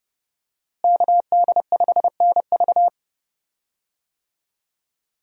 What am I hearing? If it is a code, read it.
Morse code, heard as a keyed tone, about 30 wpm, 700 Hz. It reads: KD5NV
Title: USA Call Signs - 5 Characters - Repeated 3x - 30wpm